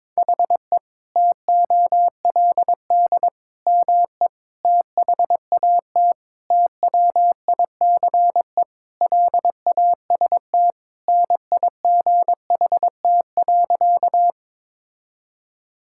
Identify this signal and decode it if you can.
Morse code, heard as a keyed tone, about 22 wpm, 700 Hz. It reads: HE TOLD ME THAT TWICE LAST NIGHT.
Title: Sentences from Top 1600 Words 22wpm